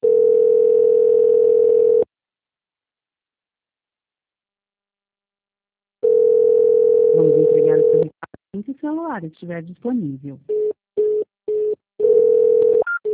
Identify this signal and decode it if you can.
Ringback tone (North America), 440 + 480 Hz, with speech.
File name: out- -2010- - - .141